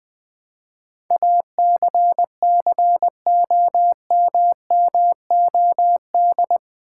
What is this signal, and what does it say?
Morse code, heard as a keyed tone, about 20 wpm, 700 Hz. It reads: ACCOMMOD
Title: Build A Word - 4 Letters At A Time - 11-Letter Words - 20wpm